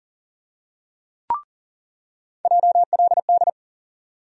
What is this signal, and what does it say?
Morse code, heard as a keyed tone, about 40 wpm, 700 Hz. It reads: JLD
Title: USA Call Sign Postfixes - 3 Letters 40wpm